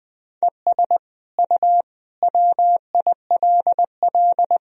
Morse code, heard as a keyed tone, about 20 wpm, 700 Hz. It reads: ES U WILL
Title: Sentences from Top 400 Words and Abbreviations 20wpm